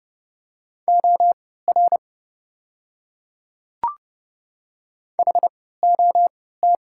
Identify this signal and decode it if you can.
Morse code, heard as a keyed tone, about 31 wpm, spaced as for 17 wpm, 700 Hz. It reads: OR HOT
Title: 3x Character Spacing - Top 100 Words - 30wpm